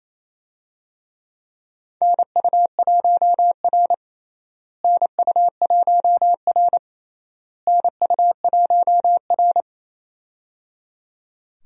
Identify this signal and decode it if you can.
Morse code, heard as a keyed tone, about 28 wpm, 700 Hz. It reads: NU1R NU1R NU1R
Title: USA Call Signs - 4 Characters - Repeated 3x - 28wpm